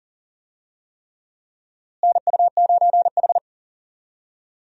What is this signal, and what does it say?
Morse code, heard as a keyed tone, about 40 wpm, 700 Hz. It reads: NU9H
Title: USA Call Signs - 4 Characters - Repeated 3x - 40wpm